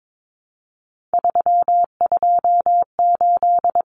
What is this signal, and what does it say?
Morse code, heard as a keyed tone, about 22 wpm, 700 Hz. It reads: 328
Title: Common Cut Numbers - 3-Digit 22wpm